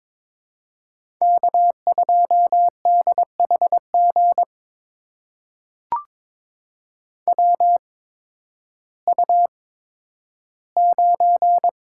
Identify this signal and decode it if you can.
Morse code, heard as a keyed tone, about 22 wpm, spaced as for 12 wpm, 700 Hz. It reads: K2DHG WU9
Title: USA Call Signs - 8x Character Spacing - 22wpm